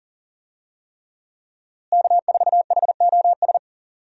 Morse code, heard as a keyed tone, about 40 wpm, 700 Hz. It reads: K4HOS
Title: USA Call Signs - 5 Characters - Repeated 3x - 40wpm